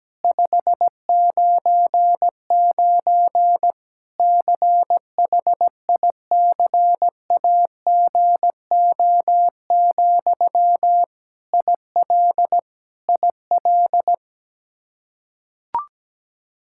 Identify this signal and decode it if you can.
Morse code, heard as a keyed tone, about 17 wpm, 700 Hz. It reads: 599 CHICAGO, IL IL
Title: Practice QSO - Exchange 1 - Beginning - 17wpm